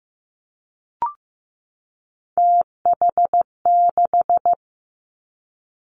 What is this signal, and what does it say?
Morse code, heard as a keyed tone, about 15 wpm, 700 Hz. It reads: TH6
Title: Three Challenging Letter-Number Combinations 15wpm